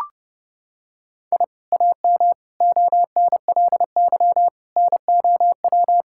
Morse code, heard as a keyed tone, about 32 wpm, 700 Hz. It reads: I AM ONLY NOW
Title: Sentences from Top 1000 Words - Review Part 5 - 30wpm